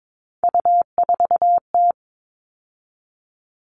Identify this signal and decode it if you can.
Morse code, heard as a keyed tone, about 22 wpm, 700 Hz. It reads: U4T
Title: All Cut Numbers - 3-Digit 22wpm